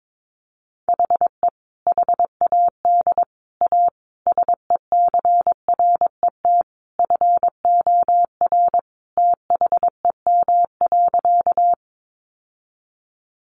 Morse code, heard as a keyed tone, about 22 wpm, 700 Hz. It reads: HE HAD A SECRET FOR THEM.
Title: Sentences from Top 1200 Words 22wpm